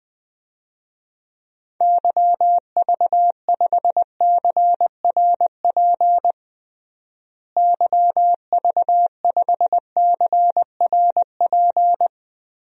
Morse code, heard as a keyed tone, about 20 wpm, 700 Hz. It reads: YV5CRP YV5CRP
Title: DXCC-100 Call Signs - 6 Characters - Repeated 3x - 20wpm